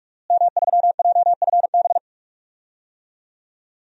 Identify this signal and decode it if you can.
Morse code, heard as a keyed tone, about 42 wpm, 700 Hz. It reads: M3JFB
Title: DXCC-100 Call Signs - 5 Characters - Repeated 2x - 45wpm